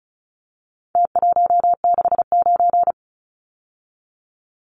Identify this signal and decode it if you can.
Morse code, heard as a keyed tone, about 34 wpm, 700 Hz. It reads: T169
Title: Numbers - Contest Style - 4-Digit 35wpm